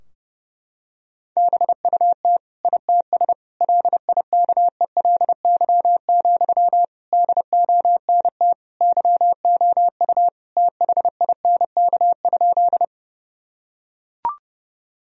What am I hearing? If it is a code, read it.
Morse code, heard as a keyed tone, about 32 wpm, 700 Hz. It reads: BUT ITS LIKELY, DONT YOU THINK?
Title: Sentences from Top 1000 Words - Review Part 4 - 30wpm